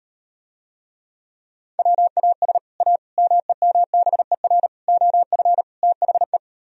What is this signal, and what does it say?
Morse code, heard as a keyed tone, about 38 wpm, 700 Hz. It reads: WAS A MEMBER OF THE